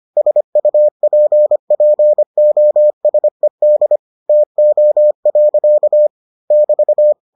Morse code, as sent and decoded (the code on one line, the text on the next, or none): ... ..- .--. .--. --- ... -.. / - --- .-.-.- / -...-
SUPPOSED TO. <BT>